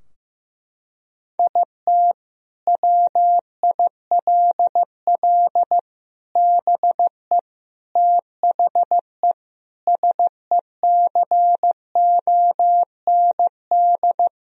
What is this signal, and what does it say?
Morse code, heard as a keyed tone, about 15 wpm, 700 Hz. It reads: IT WILL BE THE SECOND
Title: Sentences from Top 1000 Words - Review Part 2 - 15wpm